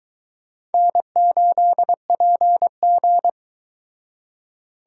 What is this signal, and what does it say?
Morse code, heard as a keyed tone, about 23 wpm, 700 Hz. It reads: N8PG